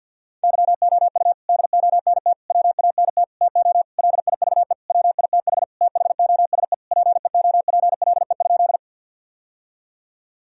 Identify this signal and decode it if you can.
Morse code, heard as a keyed tone, about 51 wpm, 700 Hz. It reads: YOU DONT WANT TO LIVE WITH THOSE PEOPLE?
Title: Sentences from Top 1000 Words - Review Part 1 - 50wpm